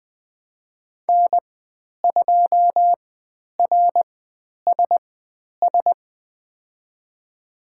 Morse code, heard as a keyed tone, about 20 wpm, spaced as for 10 wpm, 700 Hz. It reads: N2RSS
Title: 095 Lesson 16g - Letter c - Call Signs 20wpm